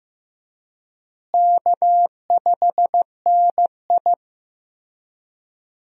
Morse code, heard as a keyed tone, about 15 wpm, 700 Hz. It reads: K5NI
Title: USA Call Signs - 4 Characters - Repeated 3x - 15wpm